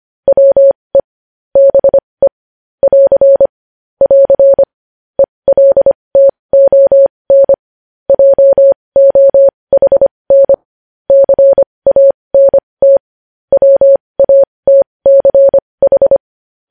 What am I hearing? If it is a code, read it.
Morse code, heard as a keyed tone, about 25 wpm, spaced as for 20 wpm, 550 Hz. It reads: WE BE <AR> <AR> ELTON JOHN CANT WATCH
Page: News headlines in morse code at 20 wpm